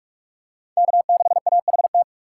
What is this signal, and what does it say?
Morse code, heard as a keyed tone, about 46 wpm, 700 Hz. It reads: K6AHT
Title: USA Call Signs - 5 Characters - 4x Character Spacing - 45wpm